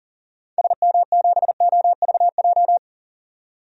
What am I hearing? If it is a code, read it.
Morse code, heard as a keyed tone, about 39 wpm, 700 Hz. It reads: SM7OVJ
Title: DXCC-100 Call Signs - 6 Characters - 40wpm